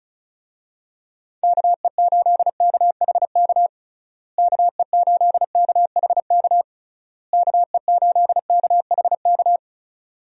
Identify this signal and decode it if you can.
Morse code, heard as a keyed tone, about 36 wpm, 700 Hz. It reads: KE8KHK KE8KHK KE8KHK
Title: USA Call Signs - 6 Characters - Repeated 3x - 35wpm